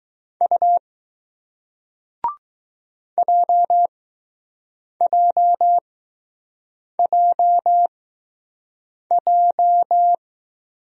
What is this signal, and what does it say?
Morse code, heard as a keyed tone, about 20 wpm, spaced as for 6 wpm, 700 Hz. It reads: U JJJJ